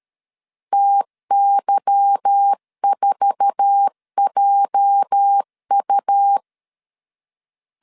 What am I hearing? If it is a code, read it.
Morse code, heard as a keyed tone, about 13 wpm, 800 Hz. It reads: TY4JU
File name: LEZIONI / 8 YJ45 / 8 YJ4 12wpm 2151